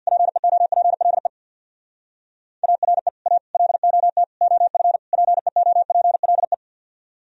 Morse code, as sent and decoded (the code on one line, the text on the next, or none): .--. --- .--. .-.. / .- .-. / .- / .-.. --- - / --- ..-. / .--. --- .--. .-.. .
PEOPLE ARE A LOT OF PEOPLE